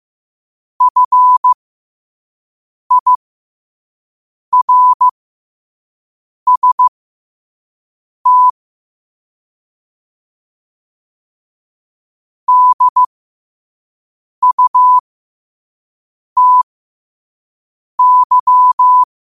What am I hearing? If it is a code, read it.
Morse code, heard as a keyed tone, about 15 wpm, spaced as for 5 wpm, 1000 Hz. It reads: FIRST DUTY
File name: qotdfiles / 5wpm